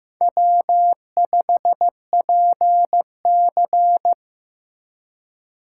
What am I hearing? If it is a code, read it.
Morse code, heard as a keyed tone, about 18 wpm, 700 Hz. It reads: W5PC
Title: Speed Racing - USA 1x2 Call Signs 15wpm